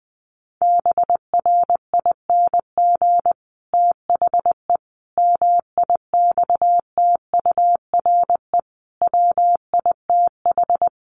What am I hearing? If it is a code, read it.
Morse code, heard as a keyed tone, about 20 wpm, 700 Hz. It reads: BRING THE MIXTURE WITH
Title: Sentences from Top 1500 Words 20wpm